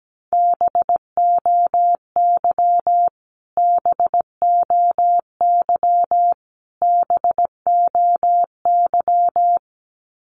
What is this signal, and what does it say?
Morse code, heard as a keyed tone, about 17 wpm, 700 Hz. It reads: BOY BOY BOY
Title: Top 500 Words Repeated 17wpm